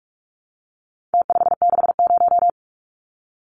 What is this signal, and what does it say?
Morse code, heard as a keyed tone, about 44 wpm, 700 Hz. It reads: T560
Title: Numbers - Contest Style - 4-Digit 45wpm